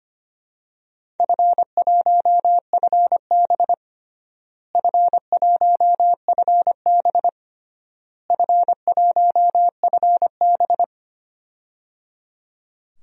Morse code, heard as a keyed tone, about 25 wpm, 700 Hz. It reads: F1FB F1FB F1FB
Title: DXCC-100 Call Signs - 4 Characters - Repeated 3x - 25wpm